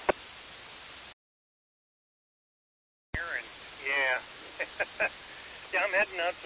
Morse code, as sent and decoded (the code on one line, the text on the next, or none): none